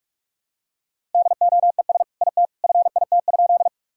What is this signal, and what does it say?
Morse code, heard as a keyed tone, about 44 wpm, 700 Hz. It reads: DOES IT FIT?
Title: Sentences from Top 1000 Words - Review Part 5 - 45wpm